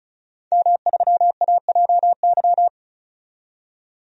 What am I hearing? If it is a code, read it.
Morse code, heard as a keyed tone, about 34 wpm, 700 Hz. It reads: M3AJY
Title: DXCC-100 Call Signs - 5 Characters - 35wpm